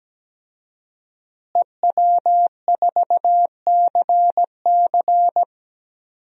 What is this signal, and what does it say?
Morse code, heard as a keyed tone, about 17 wpm, 700 Hz. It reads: EW4CC
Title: DXCC-100 Call Signs - 5 Characters - Repeated 3x - 17wpm